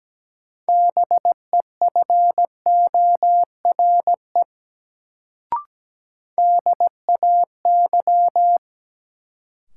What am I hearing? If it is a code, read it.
Morse code, heard as a keyed tone, about 17 wpm, 700 Hz. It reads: BEFORE DAY
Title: Sentences from Top 200 Words - New Word Drill - 17wpm